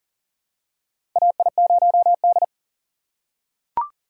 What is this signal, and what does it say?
Morse code, heard as a keyed tone, about 40 wpm, 700 Hz. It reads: AI0D
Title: USA Call Signs - 4 Characters - 3x Character Spacing - 40wpm